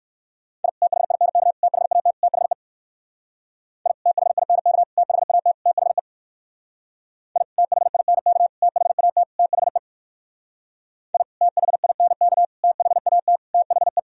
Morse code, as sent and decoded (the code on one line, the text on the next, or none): .. / - .... .. -. -.- / - .... .- - / - .... / .. / - .... .. -. -.- / - .... .- - / - .... / .. / - .... .. -. -.- / - .... .- - / - .... / .. / - .... .. -. -.- / - .... .- - / - .... .
I THINK THAT THE I THINK THAT THE I THINK THAT THE I THINK THAT THE